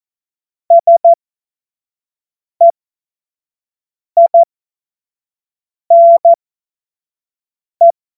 Morse code, heard as a keyed tone, about 14 wpm, spaced as for 5 wpm, 680 Hz. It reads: SEINE
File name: Morsekurs / Witze / Witz 27. Januar 2026 25 F70